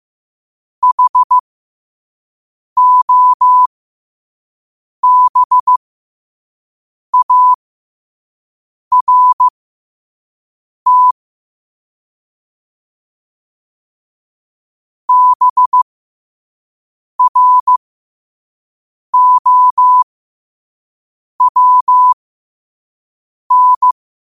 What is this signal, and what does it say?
Morse code, heard as a keyed tone, about 15 wpm, spaced as for 5 wpm, 1000 Hz. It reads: HOBART BROWN
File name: qotdfiles / 5wpm